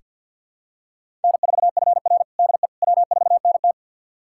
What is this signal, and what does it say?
Morse code, heard as a keyed tone, about 48 wpm, 700 Hz. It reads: N4FR DE W4NT
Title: USA Callsign DE Callsign - 4 Characters - 50wpm